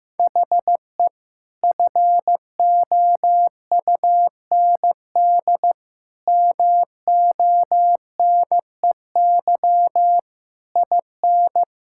Morse code, heard as a keyed tone, about 16 wpm, 700 Hz. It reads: HE FOUND MONEY IN
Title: Sentences from Top 1000 Words - Review Part 6 - 15wpm